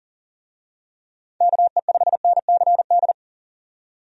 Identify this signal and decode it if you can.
Morse code, heard as a keyed tone, about 37 wpm, 700 Hz. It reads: KE5NCD
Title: USA Call Signs - 6 Characters - 40wpm